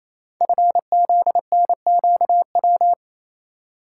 Morse code, heard as a keyed tone, about 28 wpm, 700 Hz. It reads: FZNQW
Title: Code Groups - 5 Letters - 28wpm